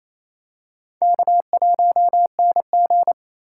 Morse code, heard as a keyed tone, about 28 wpm, 700 Hz. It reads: K1NG
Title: USA Call Signs - 4x Character Spacing - 28wpm